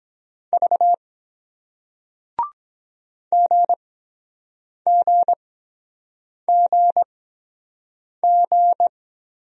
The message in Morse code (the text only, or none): V GGGG